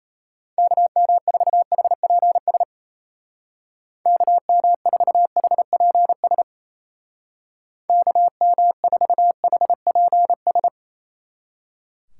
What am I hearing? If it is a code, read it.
Morse code, heard as a keyed tone, about 34 wpm, 700 Hz. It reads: KM4HPS KM4HPS KM4HPS
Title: Speed Racing - USA 2x3 Call Signs 28wpm